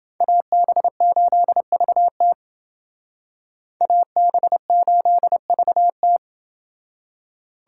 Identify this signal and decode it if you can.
Morse code, heard as a keyed tone, about 28 wpm, 700 Hz. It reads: AB8VT AB8VT